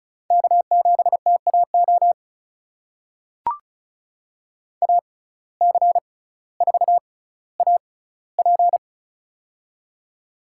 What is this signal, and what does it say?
Morse code, heard as a keyed tone, about 36 wpm, spaced as for 18 wpm, 700 Hz. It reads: K7TAO AC4AP
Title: USA Call Signs - 5 Characters - 6x Character Spacing - 35wpm